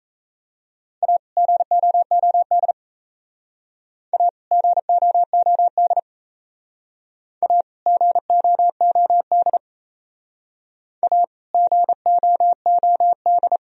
Morse code, heard as a keyed tone, about 35 wpm, 700 Hz. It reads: A GOOD A GOOD A GOOD A GOOD
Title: Speed-Racing - Sets of 2 Words 28wpm